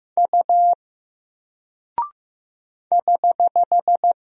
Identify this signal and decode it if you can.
Morse code, heard as a keyed tone, about 15 wpm, 700 Hz. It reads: U <HH>
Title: Single Character (Letter:Number:Punctuation:Prosign) Alt 15wpm